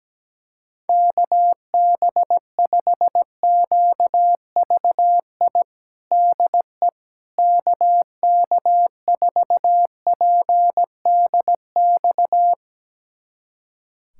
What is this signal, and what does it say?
Morse code, heard as a keyed tone, about 17 wpm, 700 Hz. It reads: KB5QVI DE KK4PDX
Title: USA Callsign DE Callsign - 6 Characters - 17wpm